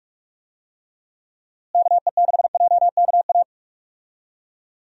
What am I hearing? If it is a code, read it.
Morse code, heard as a keyed tone, about 47 wpm, 700 Hz. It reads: KE6JKA
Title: USA Call Signs - 6 Characters - Repeated 2x - 45wpm